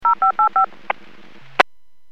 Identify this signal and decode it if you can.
DTMF, heard as telephone touch tones, tones of about 110 ms. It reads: 0585
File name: WPS / 00270 14-29 WPS 451.25 WPS